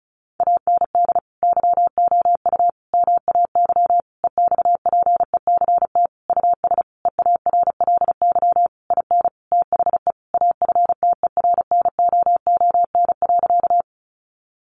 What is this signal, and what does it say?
Morse code, heard as a keyed tone, about 34 wpm, 700 Hz. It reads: AND YOU MAY EXPECT US EARLY IN THE AFTERNOON.
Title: Sentences from Top 1500 Words 35wpm